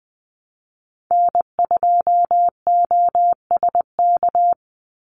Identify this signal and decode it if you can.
Morse code, heard as a keyed tone, about 20 wpm, 700 Hz. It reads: N2OSK
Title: USA Call Signs 20wpm